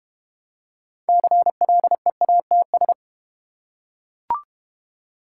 Morse code, heard as a keyed tone, about 31 wpm, 700 Hz. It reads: CLEATS
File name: Easy Words - Speed-Racing with Hints - 25wpm